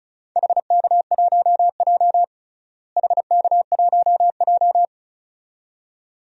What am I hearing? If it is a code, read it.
Morse code, heard as a keyed tone, about 34 wpm, 700 Hz. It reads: HK1J HK1J
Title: DXCC-100 Call Signs - 4 Characters - Repeated 2x - 35wpm